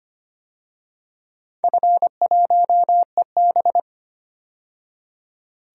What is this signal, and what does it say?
Morse code, heard as a keyed tone, about 25 wpm, 700 Hz. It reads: F1EB
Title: DXCC-100 Call Signs - 4 Characters - 25wpm